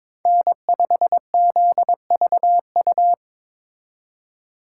Morse code, heard as a keyed tone, about 26 wpm, 700 Hz. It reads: N5ZVU
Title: Speed Racing - USA 1x3 Call Signs 22wpm